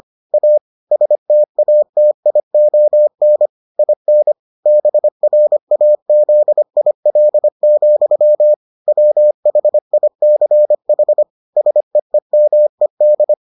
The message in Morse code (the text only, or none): A STATION IN BRAZIL, WHICH SEEMED